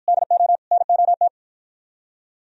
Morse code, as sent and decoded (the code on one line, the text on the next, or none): -.. --- / -. --- -
DO NOT